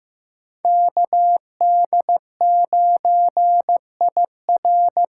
Morse code, heard as a keyed tone, about 15 wpm, 700 Hz. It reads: KD9IR
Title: USA Call Signs - 6 Characters - 8x Character Spacing - 15wpm